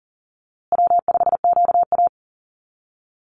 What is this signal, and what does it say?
Morse code, heard as a keyed tone, about 40 wpm, 700 Hz. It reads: W5QA
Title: USA Call Signs Repeated 2x 40wpm